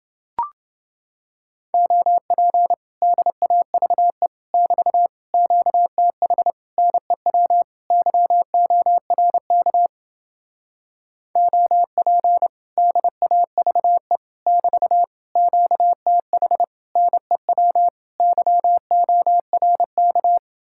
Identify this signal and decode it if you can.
Morse code, heard as a keyed tone, about 29 wpm, 700 Hz. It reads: OP DAVE <BT> QTH NEW YORK OP DAVE <BT> QTH NEW YORK